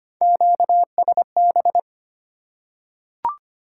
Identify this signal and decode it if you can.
Morse code, heard as a keyed tone, about 25 wpm, 690 Hz. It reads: QSB
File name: Top 90 QSO Elements - 4x Character Spacing - 25wpm